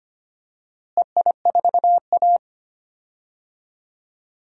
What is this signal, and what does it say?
Morse code, heard as a keyed tone, about 25 wpm, 700 Hz. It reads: EI4A
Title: DXCC-100 Call Signs - 4 Characters - Repeated 2x - 25wpm